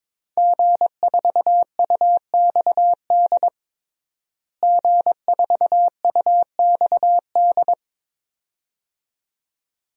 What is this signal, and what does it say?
Morse code, heard as a keyed tone, about 22 wpm, 700 Hz. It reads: G4UXD G4UXD
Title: DXCC-100 Call Signs - 5 Characters - Repeated 2x - 22wpm